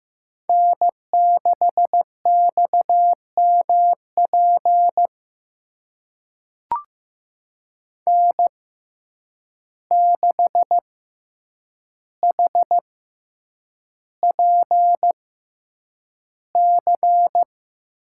Morse code, heard as a keyed tone, about 15 wpm, spaced as for 8 wpm, 700 Hz. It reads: N6XMP N6HPC
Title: USA Call Signs - 5 Characters - 6x Character Spacing - 15wpm